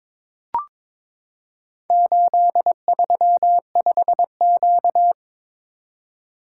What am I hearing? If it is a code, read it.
Morse code, heard as a keyed tone, about 22 wpm, 700 Hz. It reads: 835Q
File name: practice / 22wpm / Four Letter-Number Combinations - 22wpm